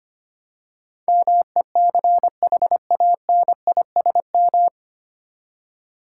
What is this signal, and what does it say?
Morse code, heard as a keyed tone, about 25 wpm, 700 Hz. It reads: MECHANISM
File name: Nine Letter Words - 25wpm